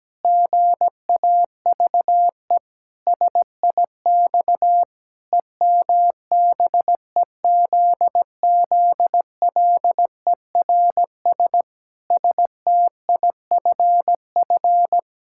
Morse code, heard as a keyed tone, about 17 wpm, 700 Hz. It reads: GAVE SIX EMBEZZLERS STIFF
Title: Pangram Challenge v2 - 17wpm